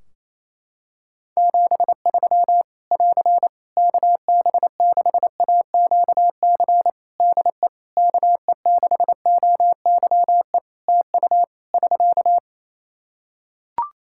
Morse code, heard as a keyed tone, about 27 wpm, 700 Hz. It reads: 73 <AR> KB6AQC DE KE6OYE TU <SK>
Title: Practice QSO - Exchange 4 - Ending - 28wpm